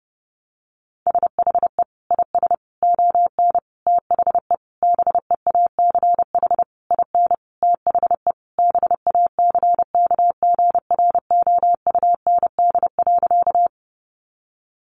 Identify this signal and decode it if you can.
Morse code, heard as a keyed tone, about 28 wpm, 700 Hz. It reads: SHE IS ON THE BEACH IN THE BACKGROUND.
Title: Sentences from Top 1500 Words 30wpm